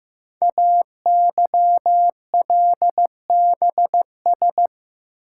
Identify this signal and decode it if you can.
Morse code, heard as a keyed tone, about 15 wpm, 700 Hz. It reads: AYLBS
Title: Code Groups - 5 Letter:Numbers - 15wpm